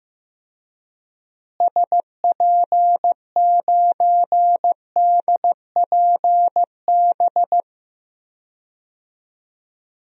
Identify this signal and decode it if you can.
Morse code, heard as a keyed tone, about 15 wpm, 700 Hz. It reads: SP9DPB
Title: DXCC-100 Call Signs - 6 Characters - Repeated 3x - 15wpm